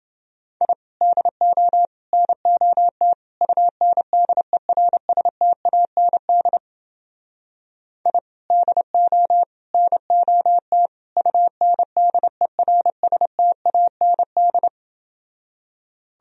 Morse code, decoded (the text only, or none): I DO NOT UNDERSTAND I DO NOT UNDERSTAND